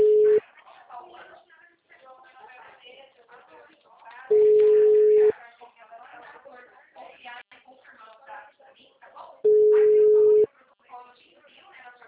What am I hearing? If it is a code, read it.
Ringback tone (Europe, 425 Hz), with speech.